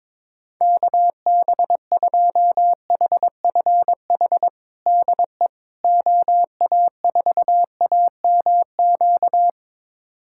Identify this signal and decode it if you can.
Morse code, heard as a keyed tone, about 22 wpm, 700 Hz. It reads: KB2HFH DE OA4AMQ
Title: USA Callsign DE Callsign - 6 Characters - 22wpm